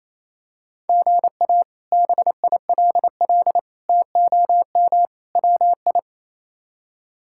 Morse code, heard as a keyed tone, about 28 wpm, 700 Hz. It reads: GA BILL TOM WI
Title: K1USN SST QSO Practice - 28wpm